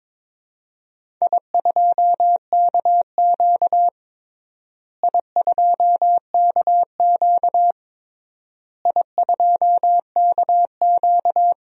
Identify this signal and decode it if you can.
Morse code, heard as a keyed tone, about 22 wpm, 700 Hz. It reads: I2KQ I2KQ I2KQ
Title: DXCC-100 Call Signs - 4 Characters - Repeated 3x - 22wpm